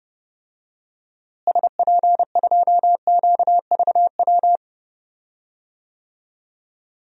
Morse code, heard as a keyed tone, about 30 wpm, 700 Hz. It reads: SP2QVW
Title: DXCC-100 Call Signs - 6 Characters - Repeated 3x - 30wpm